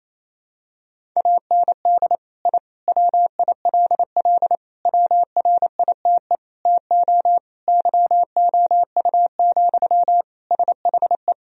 Morse code, heard as a keyed tone, about 28 wpm, 700 Hz. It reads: AND I WILL WRITE TO YOU, SHE
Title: Sentences from Top 1000 Words - Review Part 1 - 28wpm